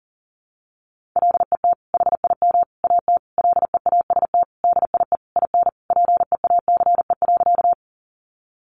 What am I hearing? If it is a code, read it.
Morse code, heard as a keyed tone, about 40 wpm, 700 Hz. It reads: LET HIM AT LEAST DIE IN PEACE.
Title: Sentences from Top 1500 Words 40wpm